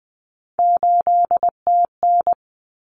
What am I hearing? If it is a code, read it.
Morse code, heard as a keyed tone, about 20 wpm, 700 Hz. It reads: 8TN
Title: Common Cut Numbers - 3-Digit 20wpm